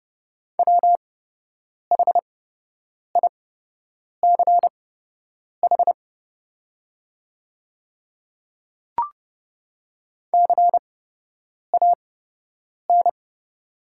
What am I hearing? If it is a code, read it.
Morse code, heard as a keyed tone, about 30 wpm, spaced as for 8 wpm, 700 Hz. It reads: WHICH CAN
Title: 8x Character Spacing - Top 100 Words - 30wpm